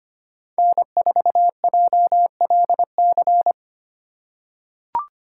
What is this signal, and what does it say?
Morse code, heard as a keyed tone, about 25 wpm, 700 Hz. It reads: N4JLC
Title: USA Call Signs - 5 Characters - 3x Character Spacing - 25wpm